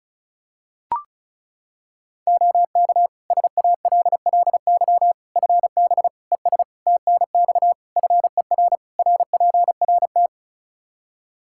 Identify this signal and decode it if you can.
Morse code, heard as a keyed tone, about 36 wpm, 700 Hz. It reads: OK SALLY FB ES TNX FER RPRT
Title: Practice QSO - Exchange 2 - Info - 35wpm